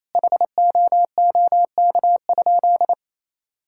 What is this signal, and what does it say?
Morse code, heard as a keyed tone, about 27 wpm, 700 Hz. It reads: HOOK?
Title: Sentences from Top 1600 Words 28wpm